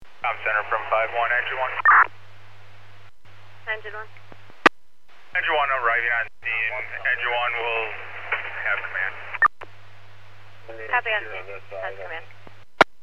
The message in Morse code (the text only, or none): none